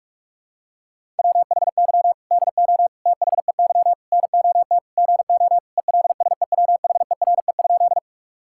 Morse code, heard as a keyed tone, about 46 wpm, 700 Hz. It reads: WHY DO THEY NOT GO ELSEWHERE?
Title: Sentences from Top 1600 Words 45wpm